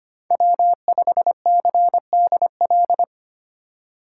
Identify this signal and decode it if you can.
Morse code, heard as a keyed tone, about 25 wpm, 700 Hz. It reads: W5CDL
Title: USA Call Signs - 6x Character Spacing - 25wpm